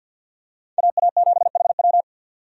Morse code, heard as a keyed tone, about 52 wpm, 700 Hz. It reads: AA7HW
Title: USA Call Signs - 5 Characters - 50wpm